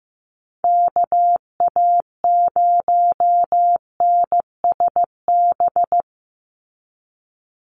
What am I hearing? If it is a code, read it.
Morse code, heard as a keyed tone, about 15 wpm, 700 Hz. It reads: KA0NSB
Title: USA Call Signs 15wpm